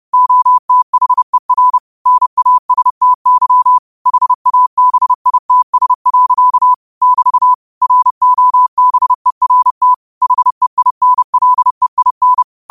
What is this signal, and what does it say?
Morse code, heard as a keyed tone, about 30 wpm, 1000 Hz. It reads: OTHER NASTY HABITS. <BT> ROBERT HEINLEIN
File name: qotdfiles / 30wpm